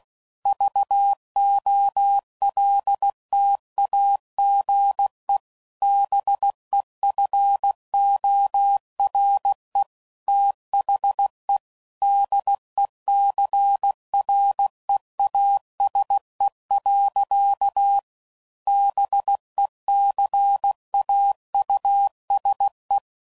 Morse code, heard as a keyed tone, about 16 wpm, 800 Hz. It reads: VOLTAGE BEFORE THE DECREASE. BECAUSE
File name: CW Code Groups @15WPM / Technical Subjects / XC